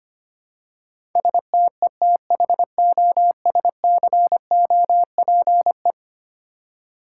Morse code, heard as a keyed tone, about 25 wpm, 700 Hz. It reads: STETHOSCOPE
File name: practice / 25wpm / Easy Words with Hints - 25wpm